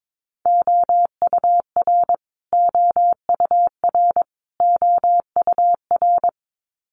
Morse code, heard as a keyed tone, about 22 wpm, 700 Hz. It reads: OUR OUR OUR
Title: Top 200 Words Repeated 22wpm